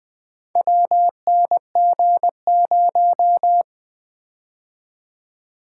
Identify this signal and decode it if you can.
Morse code, heard as a keyed tone, about 21 wpm, 700 Hz. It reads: WNG0
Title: Four Letter-Number Combinations - 20wpm